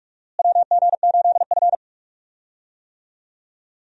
Morse code, heard as a keyed tone, about 47 wpm, 700 Hz. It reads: WG8F
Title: USA Call Signs - 4 Characters - Repeated 2x - 45wpm